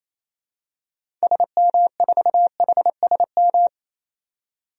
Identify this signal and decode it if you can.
Morse code, heard as a keyed tone, about 28 wpm, 700 Hz. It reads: SM4HSM
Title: DXCC-100 Call Signs - 6 Characters - 28wpm